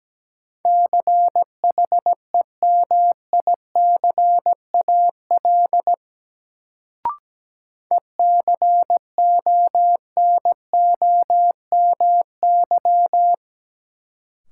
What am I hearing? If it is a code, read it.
Morse code, heard as a keyed tone, about 17 wpm, 700 Hz. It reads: CHEMICAL ECONOMY
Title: Sentences from Top 1200 Words - New Word Drill Alt - 17wpm